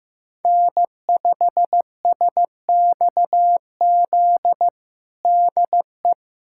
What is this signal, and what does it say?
Morse code, heard as a keyed tone, about 15 wpm, 700 Hz. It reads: N5SXZ DE
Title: USA Callsign DE Callsign 15wpm